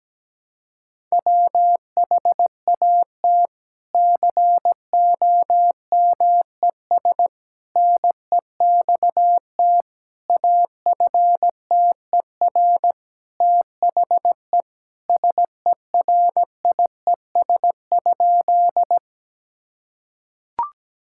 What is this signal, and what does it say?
Morse code, heard as a keyed tone, about 17 wpm, 700 Hz. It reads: WHAT COMES NEXT AFTER THE SERIES?
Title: Sentences from Top 1000 Words - Review Part 5 - 17wpm